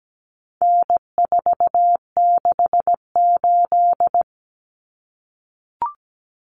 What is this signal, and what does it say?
Morse code, heard as a keyed tone, about 17 wpm, 700 Hz. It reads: N468
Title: Common Cut Numbers - 4-Digit 17wpm